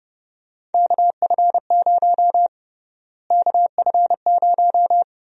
Morse code, heard as a keyed tone, about 30 wpm, 700 Hz. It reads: KF0 KF0
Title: USA Call Sign Prefixes - Repeated 2x - 30wpm